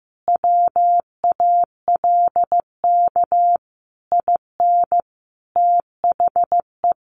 Morse code, heard as a keyed tone, about 15 wpm, 700 Hz. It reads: WALK IN THE
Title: Sentences from Top 1400 Words 15wpm